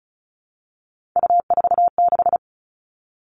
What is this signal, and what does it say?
Morse code, heard as a keyed tone, about 36 wpm, 700 Hz. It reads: U46
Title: Three Challenging Letter-Number Combinations 35wpm